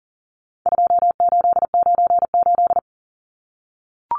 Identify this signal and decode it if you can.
Morse code, heard as a keyed tone, about 40 wpm, 700 Hz. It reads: 2898
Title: Numbers - 4-Digit 40wpm